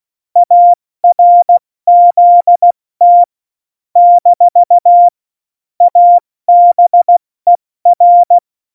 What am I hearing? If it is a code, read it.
Morse code, heard as a keyed tone, about 16 wpm, 700 Hz. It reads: ARZT - ABER
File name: Morsekurs / Witze / Witz 06 March 2018, 70 F80